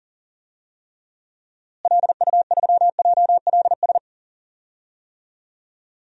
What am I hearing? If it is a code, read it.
Morse code, heard as a keyed tone, about 40 wpm, 700 Hz. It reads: LU3JLS
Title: DXCC-100 Call Signs - 6 Characters - Repeated 3x - 40wpm